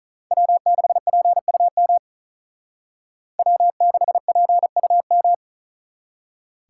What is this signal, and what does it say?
Morse code, heard as a keyed tone, about 45 wpm, 700 Hz. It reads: W6PUM W6PUM